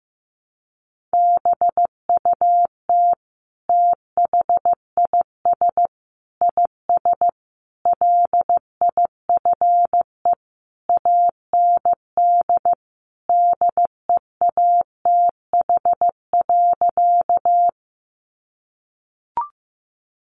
Morse code, heard as a keyed tone, about 15 wpm, 700 Hz. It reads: BUT THIS IS LIFE AND DEATH.
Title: Sentences from Top 1100 Words 15wpm